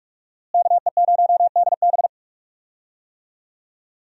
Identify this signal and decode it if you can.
Morse code, heard as a keyed tone, about 46 wpm, 700 Hz. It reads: KE0DB
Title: USA Call Signs - 5 Characters - Repeated 3x - 45wpm